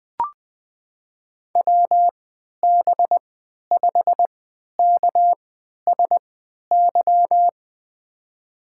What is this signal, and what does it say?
Morse code, heard as a keyed tone, about 21 wpm, spaced as for 12 wpm, 700 Hz. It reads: WB5KSY
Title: USA Call Signs - 6 Characters - 3x Character Spacing - 20wpm